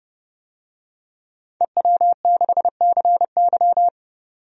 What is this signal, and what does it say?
Morse code, heard as a keyed tone, about 30 wpm, 700 Hz. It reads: EW6CY
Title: DXCC-100 Call Signs - 5 Characters - 30wpm